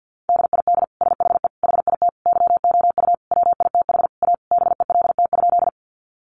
Morse code, heard as a keyed tone, about 50 wpm, 700 Hz. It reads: DID SHE HIT YOU WITH A BELT?
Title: Sentences from Top 1400 Words 50wpm